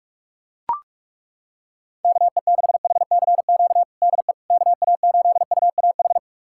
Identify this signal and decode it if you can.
Morse code, heard as a keyed tone, about 42 wpm, 700 Hz. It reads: KE6HCQ DE KA8UAH
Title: USA Callsign DE Callsign 45wpm